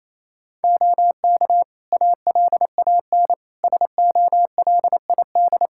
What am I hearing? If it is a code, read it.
Morse code, heard as a keyed tone, about 28 wpm, 700 Hz. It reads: OK ALAN SOLID
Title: Practice QSO - Exchange 3 - More Info - 28wpm